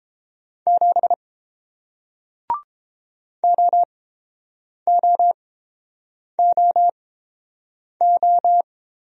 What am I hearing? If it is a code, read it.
Morse code, heard as a keyed tone, about 29 wpm, spaced as for 7 wpm, 700 Hz. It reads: 7 OOOO